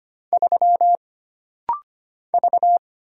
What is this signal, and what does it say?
Morse code, heard as a keyed tone, about 25 wpm, 700 Hz. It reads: 3 V
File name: Single Character (Letter:Number:Punctuation:Prosign) - Rapid-Fire - 25wpm